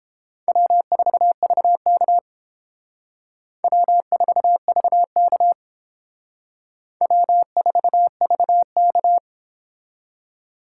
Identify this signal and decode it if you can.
Morse code, heard as a keyed tone, about 31 wpm, 700 Hz. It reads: W4VK W4VK W4VK